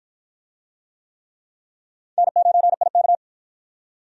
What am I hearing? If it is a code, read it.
Morse code, heard as a keyed tone, about 53 wpm, 700 Hz. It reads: N9IK